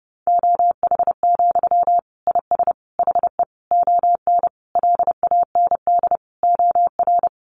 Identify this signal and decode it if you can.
Morse code, heard as a keyed tone, about 30 wpm, 700 Hz. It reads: OH, IS HE ON LAND OR
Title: Sentences from Top 500 Words 30wpm